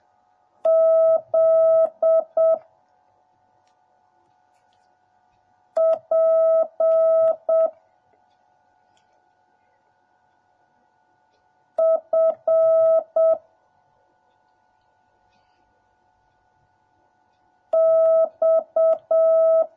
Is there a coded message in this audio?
Yes: Morse code from 0.7 to 19.6 s, heard as a keyed tone, about 7 wpm, 650 Hz. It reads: Z P F X